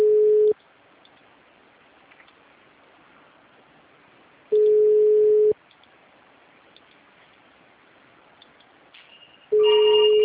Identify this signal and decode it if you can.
Ringback tone (Europe, 425 Hz).
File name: out- -2017- - - .16017